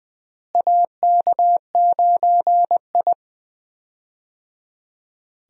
Morse code, heard as a keyed tone, about 21 wpm, 700 Hz. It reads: AK9I
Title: USA Call Signs - 4 Characters - Repeated 3x - 20wpm